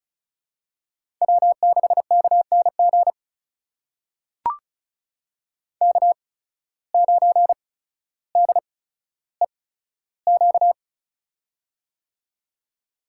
Morse code, heard as a keyed tone, about 36 wpm, spaced as for 15 wpm, 700 Hz. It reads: W6KNG K9DEQ
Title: USA Call Signs - 5 Characters - 8x Character Spacing - 35wpm